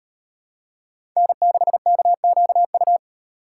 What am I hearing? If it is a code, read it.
Morse code, heard as a keyed tone, about 32 wpm, 700 Hz. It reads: N6KQU